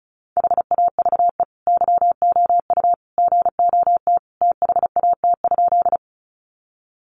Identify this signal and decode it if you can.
Morse code, heard as a keyed tone, about 34 wpm, 700 Hz. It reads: HAVE YOU GOT THAT?
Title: Sentences from Top 300 Words 35wpm